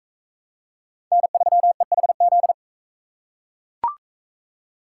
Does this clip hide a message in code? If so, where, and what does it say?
Yes: Morse code from 1.1 to 2.5 s, heard as a keyed tone, about 38 wpm, 700 Hz. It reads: N3EHZ